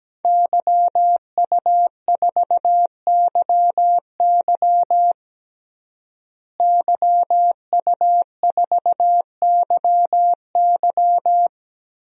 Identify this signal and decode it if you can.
Morse code, heard as a keyed tone, about 17 wpm, 700 Hz. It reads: YU4YY YU4YY
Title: DXCC-100 Call Signs - 5 Characters - Repeated 3x - 17wpm